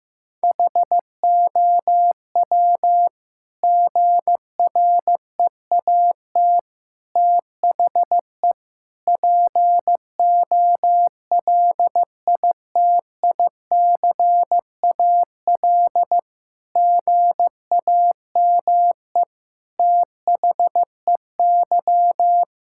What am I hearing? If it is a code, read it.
Morse code, heard as a keyed tone, about 16 wpm, 700 Hz. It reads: HOW GREAT THE POLITICAL GAME THEY
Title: Sentences from Top 1000 Words - Review Part 5 - 15wpm